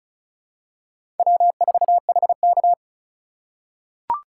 Morse code, heard as a keyed tone, about 36 wpm, 700 Hz. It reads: W4HK
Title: USA Call Signs - 4 Characters - 3x Character Spacing - 35wpm